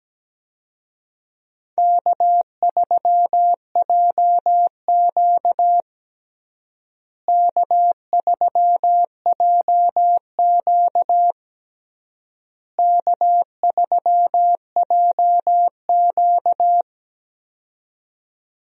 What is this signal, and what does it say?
Morse code, heard as a keyed tone, about 17 wpm, 700 Hz. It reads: K3JQ K3JQ K3JQ
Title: USA Call Signs - 4 Characters - Repeated 3x - 17wpm